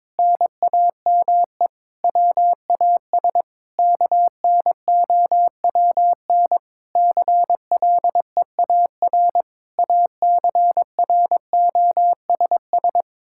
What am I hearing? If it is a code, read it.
Morse code, heard as a keyed tone, about 22 wpm, 700 Hz. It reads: NAME WAS KNOWN CLEAR ACROSS
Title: Sentences from Top 1000 Words - Review Part 5 - 22wpm